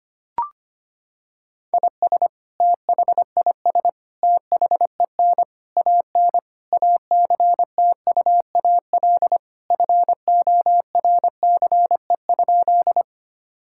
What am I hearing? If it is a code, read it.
Morse code, heard as a keyed tone, about 25 wpm, 700 Hz. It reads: IS THIS THEN AN ACTUAL FORCE?
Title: Sentences from Top 1000 Words - Review Part 5 - 25wpm